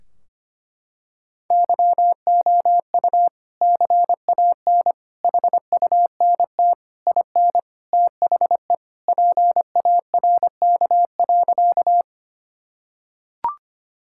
Morse code, heard as a keyed tone, about 25 wpm, 700 Hz. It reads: YOU CAN HUNT IN THE PARK.
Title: Sentences from Top 1600 Words 25wpm